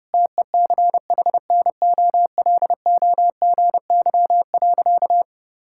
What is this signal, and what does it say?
Morse code, heard as a keyed tone, about 30 wpm, 700 Hz. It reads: TECHNOLOGY.
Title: Sentences from Top 1000 Words - Review Part 6 - 30wpm